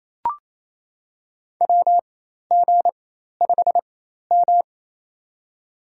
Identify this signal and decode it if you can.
Morse code, heard as a keyed tone, about 28 wpm, spaced as for 13 wpm, 700 Hz. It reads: WG5M
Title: USA Call Signs - 4x Character Spacing - 28wpm